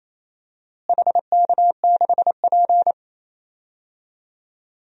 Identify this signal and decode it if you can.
Morse code, heard as a keyed tone, about 28 wpm, 700 Hz. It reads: HK6P
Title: DXCC-100 Call Signs - 4 Characters - Repeated 3x - 28wpm